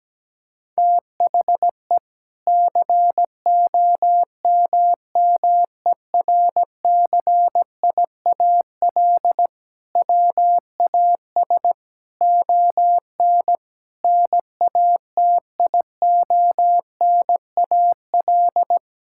Morse code, heard as a keyed tone, about 17 wpm, 700 Hz. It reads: THE COMMERCIAL WAS ON NATIONAL
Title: Sentences from Top 1000 Words - Review Part 5 - 17wpm